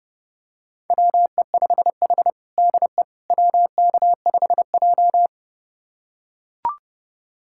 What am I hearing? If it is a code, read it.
Morse code, heard as a keyed tone, about 30 wpm, 700 Hz. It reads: WE5H DE WK5J
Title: USA Callsign DE Callsign - 4 Characters - 30wpm